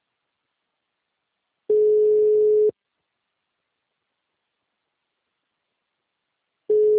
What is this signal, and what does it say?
Ringback tone (Europe, 425 Hz).